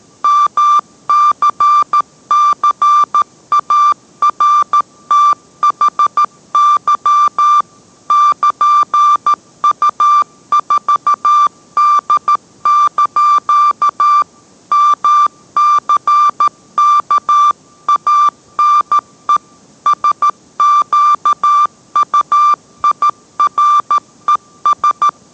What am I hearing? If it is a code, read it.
Morse code, heard as a keyed tone, about 13 wpm, 1180 Hz. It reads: MCCARTHY <KN>U4D) MCKANE SQUIRES